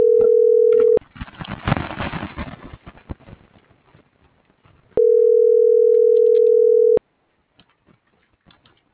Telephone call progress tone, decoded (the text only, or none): ringback tone